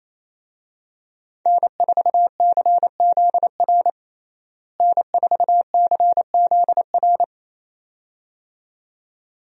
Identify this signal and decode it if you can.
Morse code, heard as a keyed tone, about 28 wpm, 700 Hz. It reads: N4CZR N4CZR
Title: USA Call Signs - 5 Characters - Repeated 2x - 28wpm